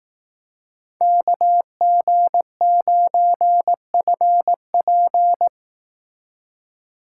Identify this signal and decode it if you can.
Morse code, heard as a keyed tone, about 18 wpm, 700 Hz. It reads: KG9FP
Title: Speed Racing - USA Call Signs 15wpm